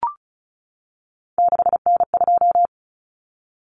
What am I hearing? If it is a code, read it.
Morse code, heard as a keyed tone, about 34 wpm, 700 Hz. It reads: 6N2
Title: Common Cut Numbers - 3-Digit 35wpm